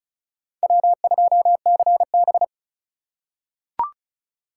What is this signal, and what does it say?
Morse code, heard as a keyed tone, about 36 wpm, 700 Hz. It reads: W2CB
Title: USA Call Signs - 4 Characters - 2x Character Spacing - 35wpm